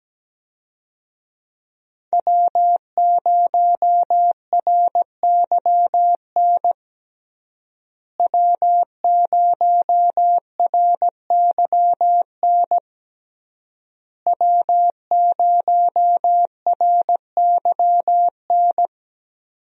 Morse code, heard as a keyed tone, about 17 wpm, 700 Hz. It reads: W0RYN W0RYN W0RYN
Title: USA Call Signs - 5 Characters - Repeated 3x - 17wpm